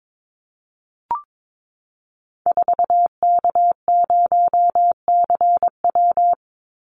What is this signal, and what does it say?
Morse code, heard as a keyed tone, about 22 wpm, 690 Hz. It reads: 4K0CW
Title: Call Signs Repeated 3x - 22wpm